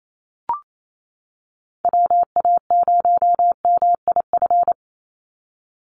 Morse code, heard as a keyed tone, about 28 wpm, 700 Hz. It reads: WA0MIF
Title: USA Call Signs 28wpm